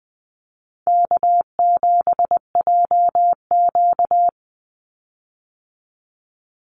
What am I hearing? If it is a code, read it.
Morse code, heard as a keyed tone, about 20 wpm, 700 Hz. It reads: K7JQ
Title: Easy Call Signs - 20wpm